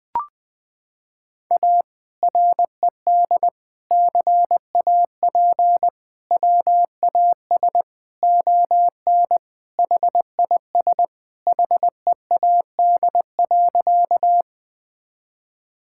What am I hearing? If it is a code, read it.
Morse code, heard as a keyed tone, about 20 wpm, 700 Hz. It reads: A RED CAP WAS ON HIS HEAD.
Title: Sentences from Top 1600 Words 20wpm